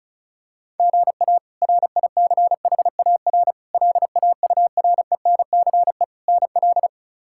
Morse code, heard as a keyed tone, about 34 wpm, 700 Hz. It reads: GA RICHAR LAURENCE NL
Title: K1USN SST QSO Practice - 35wpm